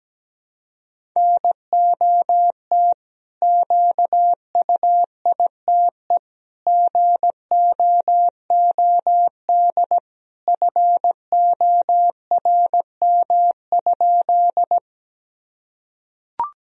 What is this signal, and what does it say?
Morse code, heard as a keyed tone, about 17 wpm, 700 Hz. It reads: NOT QUITE GOOD FORM?
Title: Sentences from Top 1000 Words - Review Part 1 - 17wpm